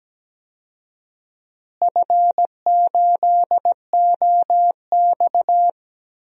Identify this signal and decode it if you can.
Morse code, heard as a keyed tone, about 17 wpm, 700 Hz. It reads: F8OX
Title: DXCC-100 Call Signs - 4 Characters - Repeated 2x - 17wpm